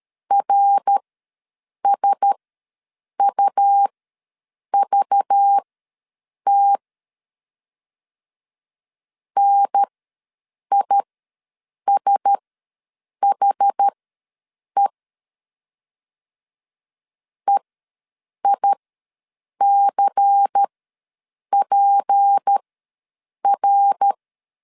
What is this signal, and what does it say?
Morse code, heard as a keyed tone, about 13 wpm, spaced as for 7 wpm, 800 Hz. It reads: RSUVT NISHE EICPR
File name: LEZIONI / 6 PKX0 / 6 P ws 12wpm 3117